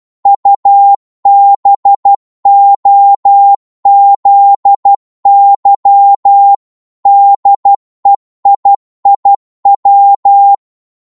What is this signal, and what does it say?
Morse code, heard as a keyed tone, about 12 wpm, 800 Hz. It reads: UBOZY DEIIW
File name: Morse / ltr-12wpm